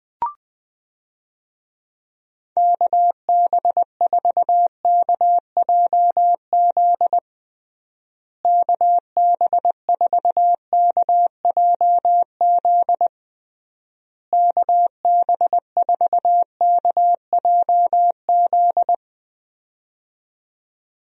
Morse code, heard as a keyed tone, about 20 wpm, 700 Hz. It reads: KB4KJZ KB4KJZ KB4KJZ
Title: USA Call Signs - 6 Characters - Repeated 3x - 20wpm